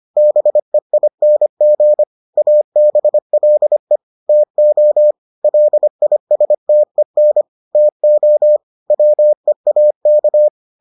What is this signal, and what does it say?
Morse code, heard as a keyed tone, about 25 wpm, 600 Hz. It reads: BEING ABLE TO LISTEN TO WEAK